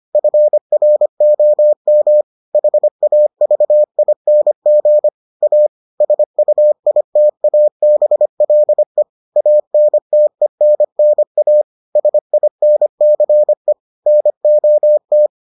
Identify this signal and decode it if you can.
Morse code, heard as a keyed tone, about 25 wpm, 600 Hz. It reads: FROM HAVING A SUITABLE ANTENNA SINCE NOT